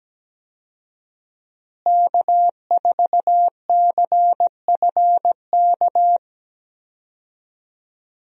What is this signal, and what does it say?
Morse code, heard as a keyed tone, about 17 wpm, 700 Hz. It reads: K4CFK
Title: USA Call Signs - 5 Characters - Repeated 3x - 17wpm